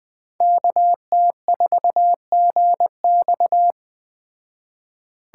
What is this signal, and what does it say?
Morse code, heard as a keyed tone, about 21 wpm, 700 Hz. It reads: KT4GX
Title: Speed Racing - USA 2x2 Call Signs 17wpm